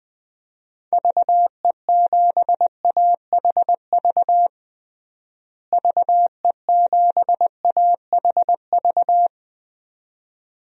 Morse code, heard as a keyed tone, about 20 wpm, 700 Hz. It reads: VE7AHV VE7AHV
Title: DXCC-100 Call Signs - 6 Characters - Repeated 2x - 20wpm